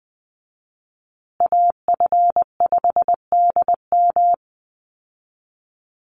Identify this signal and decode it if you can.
Morse code, heard as a keyed tone, about 20 wpm, 700 Hz. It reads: AF5DM
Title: Call Signs Repeated 2x - 20wpm